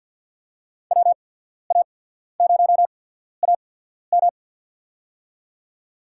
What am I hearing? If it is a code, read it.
Morse code, heard as a keyed tone, about 53 wpm, spaced as for 14 wpm, 700 Hz. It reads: WA0AM
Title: USA Call Signs - 5 Characters - 8x Character Spacing - 50wpm